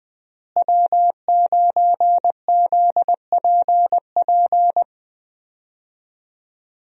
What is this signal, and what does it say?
Morse code, heard as a keyed tone, about 20 wpm, 700 Hz. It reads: W9ZPP
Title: USA Call Signs - 5 Characters - Repeated 3x - 20wpm